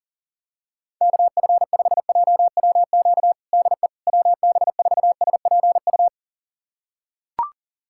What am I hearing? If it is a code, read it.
Morse code, heard as a keyed tone, about 40 wpm, 700 Hz. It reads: KF5JWQ DE WB4SPU
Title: USA Callsign DE Callsign - 6 Characters - 40wpm